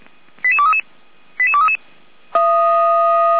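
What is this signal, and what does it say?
Five-tone selective call (ZVEI), heard as quick runs of five tones, tones of about 80 ms. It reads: 88233 88233
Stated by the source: ZVEI five-tone call received by SDR, from the feuerwehr-orlamuende.org (f85360) audio archive.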